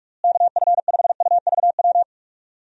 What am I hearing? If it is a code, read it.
Morse code, heard as a keyed tone, about 46 wpm, 700 Hz. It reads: KF5UVW
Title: USA Call Signs - 6 Characters - Repeated 2x - 45wpm